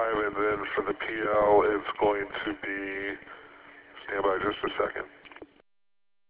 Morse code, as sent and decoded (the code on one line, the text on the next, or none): none